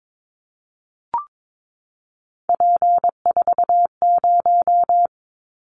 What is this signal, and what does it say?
Morse code, heard as a keyed tone, about 22 wpm, 700 Hz. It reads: P40
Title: Callsign Prefixes for Top 100 DXCC - 22wpm